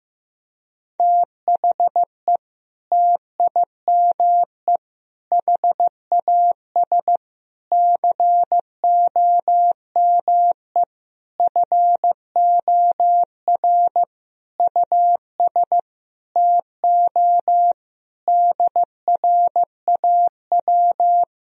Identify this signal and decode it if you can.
Morse code, heard as a keyed tone, about 16 wpm, 700 Hz. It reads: THE TIME HAS COME FOR US TO DRAW
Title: Sentences from Top 1000 Words - Review Part 1 - 15wpm